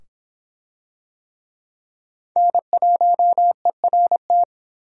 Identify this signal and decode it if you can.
Morse code, heard as a keyed tone, about 23 wpm, 700 Hz. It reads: N1ERT